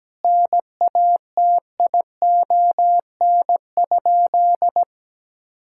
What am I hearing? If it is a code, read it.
Morse code, heard as a keyed tone, about 17 wpm, 700 Hz. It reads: NATION?
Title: Sentences from Top 1000 Words - Review Part 6 - 17wpm